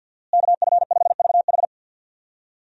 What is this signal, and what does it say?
Morse code, heard as a keyed tone, about 52 wpm, 700 Hz. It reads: KF5VH
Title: USA Call Signs - 5 Characters - Repeated 3x - 50wpm